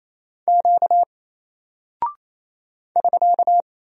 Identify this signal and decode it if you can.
Morse code, heard as a keyed tone, about 28 wpm, 700 Hz. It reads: Q <SK>
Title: Single Character (Letter:Number:Punctuation:Prosign) Alt 28wpm